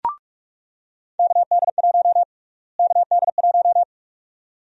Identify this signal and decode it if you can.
Morse code, heard as a keyed tone, about 45 wpm, 700 Hz. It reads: KD1 KD1
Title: USA Call Sign Prefixes - Repeated 2x - 45wpm